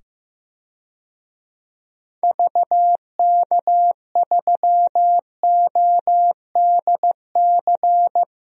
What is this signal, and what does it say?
Morse code, heard as a keyed tone, about 15 wpm, 700 Hz. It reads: VK3ODC
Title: DXCC-100 Call Signs - 6 Characters - 15wpm